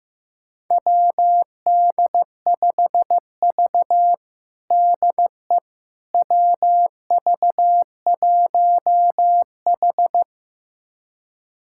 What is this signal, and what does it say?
Morse code, heard as a keyed tone, about 15 wpm, 700 Hz. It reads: WD5V DE WV1H
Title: USA Callsign DE Callsign - 4 Characters - 15wpm